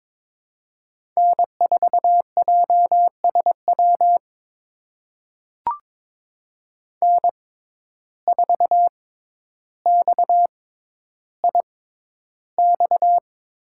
Morse code, heard as a keyed tone, about 22 wpm, spaced as for 11 wpm, 700 Hz. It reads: N4JSW N4XIX
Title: USA Call Signs - 5 Characters - 6x Character Spacing - 22wpm